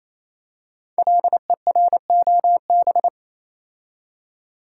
Morse code, heard as a keyed tone, about 28 wpm, 700 Hz. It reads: LEROB
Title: Code Groups - 5 Letters - 28wpm